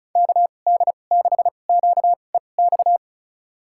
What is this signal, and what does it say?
Morse code, heard as a keyed tone, about 36 wpm, spaced as for 25 wpm, 700 Hz. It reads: KD6QEX
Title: USA Call Signs - 6 Characters - 2x Character Spacing - 35wpm